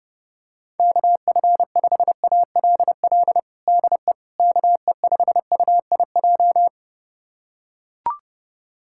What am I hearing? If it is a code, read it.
Morse code, heard as a keyed tone, about 30 wpm, 700 Hz. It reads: KF5ALL DE KE5UIJ
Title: USA Callsign DE Callsign 30wpm